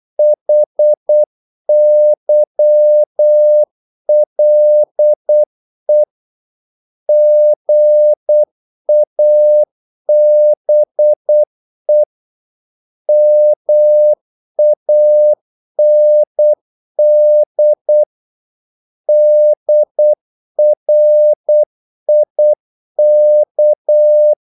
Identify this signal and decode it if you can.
Morse code, heard as a keyed tone, about 8 wpm, 600 Hz. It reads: HYLE GABE MAND DRIK
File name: Korte ord DK 8wpm